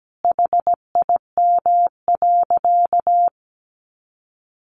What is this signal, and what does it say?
Morse code, heard as a keyed tone, about 17 wpm, 700 Hz. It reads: HIM.
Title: Easy Phrases 17wpm